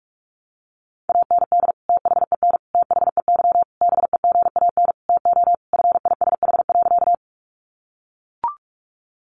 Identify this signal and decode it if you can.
Morse code, heard as a keyed tone, about 44 wpm, 700 Hz. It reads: AND THEN THEY BEGAN TO FISH.
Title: Sentences from Top 300 Words 45wpm